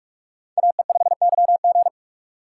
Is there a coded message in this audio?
Yes: Morse code from 0.6 to 1.9 s, heard as a keyed tone, about 45 wpm, 700 Hz. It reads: AE5YG